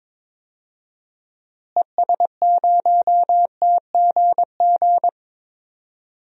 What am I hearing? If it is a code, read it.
Morse code, heard as a keyed tone, about 22 wpm, 700 Hz. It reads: ES0TGG
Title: DXCC-100 Call Signs - 6 Characters - Repeated 3x - 22wpm